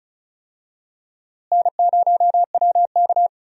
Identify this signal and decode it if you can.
Morse code, heard as a keyed tone, about 36 wpm, 700 Hz. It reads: N0WK
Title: USA Call Signs - 4 Characters - Repeated 2x - 35wpm